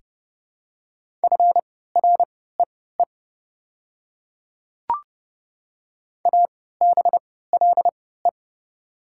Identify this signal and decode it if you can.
Morse code, heard as a keyed tone, about 30 wpm, spaced as for 17 wpm, 690 Hz. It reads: FREE ABLE
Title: 3x Character Spacing - Top 500 Words - 30wpm